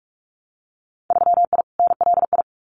Morse code, heard as a keyed tone, about 46 wpm, 700 Hz. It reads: 3I NLI
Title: Winter Field Day Exchanges Encore 45wpm